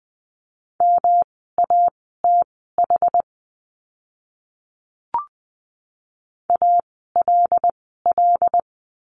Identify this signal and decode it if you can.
Morse code, heard as a keyed tone, about 20 wpm, spaced as for 15 wpm, 700 Hz. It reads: MATH ALL
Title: 108 Lesson 18f - Letter m - Words 20wpm